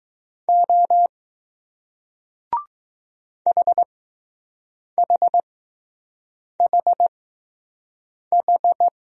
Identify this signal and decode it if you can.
Morse code, heard as a keyed tone, about 20 wpm, spaced as for 6 wpm, 700 Hz. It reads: O HHHH